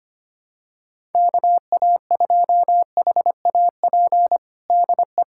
Morse code, heard as a keyed tone, about 25 wpm, 700 Hz. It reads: KA2HAP DE
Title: USA Callsign DE Callsign - 6 Characters - 25wpm